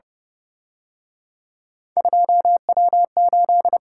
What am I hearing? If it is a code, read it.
Morse code, heard as a keyed tone, about 30 wpm, 700 Hz. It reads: 2W8